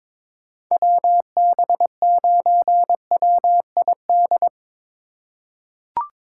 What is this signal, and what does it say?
Morse code, heard as a keyed tone, about 22 wpm, 700 Hz. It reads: WB9WID
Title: USA Call Signs - 6 Characters - 2x Character Spacing - 22wpm